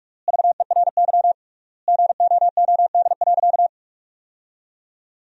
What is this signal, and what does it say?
Morse code, heard as a keyed tone, about 47 wpm, 700 Hz. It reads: VERY GOOD.
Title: 3x Word-Spacing - Sentences from Top 200 Words - 45wpm